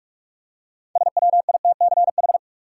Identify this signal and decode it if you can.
Morse code, heard as a keyed tone, about 46 wpm, 700 Hz. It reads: SWITCH
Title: Six Letter Words - 45wpm